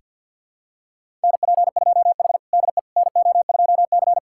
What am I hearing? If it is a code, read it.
Morse code, heard as a keyed tone, about 49 wpm, 700 Hz. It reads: NP2H DE NO2C
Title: USA Callsign DE Callsign - 4 Characters - 50wpm